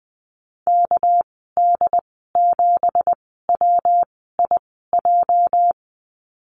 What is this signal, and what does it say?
Morse code, heard as a keyed tone, about 20 wpm, spaced as for 14 wpm, 700 Hz. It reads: KD7WIJ
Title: 207 Lesson 33g - Letter j - Call Signs 20wpm